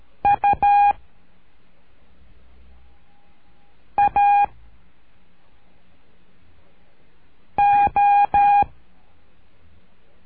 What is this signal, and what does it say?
Morse code, heard as a keyed tone, about 13 wpm, 810 Hz. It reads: U A O
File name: cursos / CURSO DE TELEGRAFIA / Telegrafia - Trilha 01